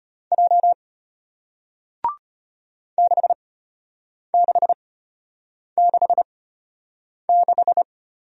Morse code, heard as a keyed tone, about 33 wpm, spaced as for 8 wpm, 700 Hz. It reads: J 6666